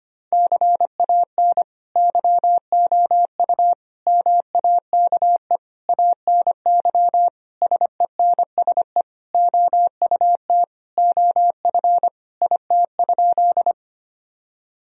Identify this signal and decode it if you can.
Morse code, heard as a keyed tone, about 25 wpm, 700 Hz. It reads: CAN YOU MAKE ANY SENSE OUT OF IT?
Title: Sentences from Top 1000 Words - Review Part 4 - 25wpm